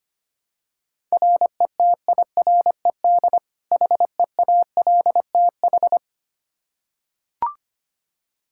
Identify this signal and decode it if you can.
Morse code, heard as a keyed tone, about 25 wpm, 700 Hz. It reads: RETIRED HEALTH
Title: Practice QSO - Exchange 3 - More Info - 25wpm